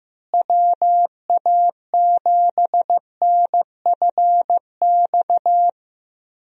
Morse code, heard as a keyed tone, about 15 wpm, 700 Hz. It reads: WA7NFX
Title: USA Call Signs - 6 Characters - Repeated 2x - 15wpm